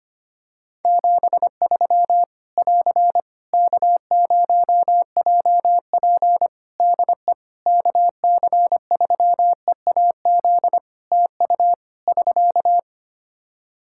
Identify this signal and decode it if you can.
Morse code, heard as a keyed tone, about 25 wpm, 700 Hz. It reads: 73 <AR> K0JP DE KC3EAZ TU <SK>
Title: Practice QSO - Exchange 4 - Ending - 25wpm